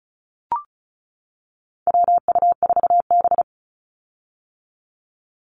Morse code, heard as a keyed tone, about 36 wpm, 700 Hz. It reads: WU4B
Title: Call Signs - 35wpm